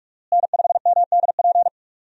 Morse code, heard as a keyed tone, about 44 wpm, 700 Hz. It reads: N5MDP
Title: USA Call Signs - 5 Characters - Repeated 3x - 45wpm